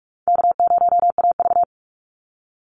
Morse code, heard as a keyed tone, about 44 wpm, 700 Hz. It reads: K0AV
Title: Easy Call Signs Repeated 3x - 45wpm